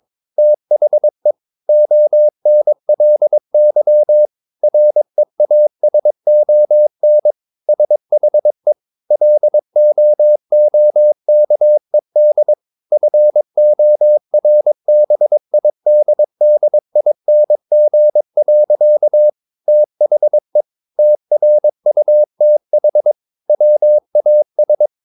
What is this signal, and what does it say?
Morse code, heard as a keyed tone, about 22 wpm, 600 Hz. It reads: THE ONLY REASON SHE LOOKED FORBIDDING. THE TRUTH WAS
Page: Romanen Never af Ken Follet på engelsk.